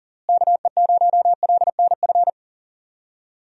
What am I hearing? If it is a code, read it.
Morse code, heard as a keyed tone, about 40 wpm, 700 Hz. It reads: KE0LNF
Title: USA Call Signs - 6 Characters - Repeated 3x - 40wpm